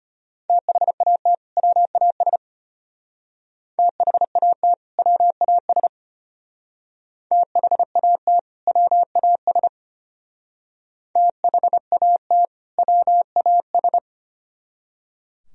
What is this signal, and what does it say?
Morse code, heard as a keyed tone, about 29 wpm, 700 Hz. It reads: THAT WAS THAT WAS THAT WAS THAT WAS
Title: Speed-Racing - Sets of 2 Words 25wpm